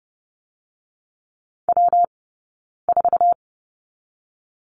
Morse code, heard as a keyed tone, about 30 wpm, 700 Hz. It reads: W 4
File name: Lesson 19d - Letter w - Two Characters 30wpm